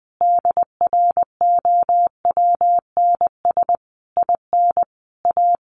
Morse code, heard as a keyed tone, about 20 wpm, 700 Hz. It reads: DROWNS IN A
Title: A Sound Of Thunder 20wpm